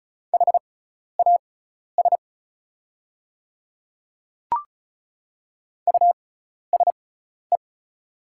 Morse code, heard as a keyed tone, about 36 wpm, spaced as for 12 wpm, 690 Hz. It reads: HAS USE
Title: 6x Character Spacing - Top 100 Words - 35wpm